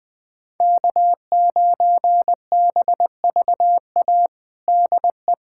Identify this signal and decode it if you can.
Morse code, heard as a keyed tone, about 20 wpm, 700 Hz. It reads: K9BVA DE
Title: USA Callsign DE Callsign 20wpm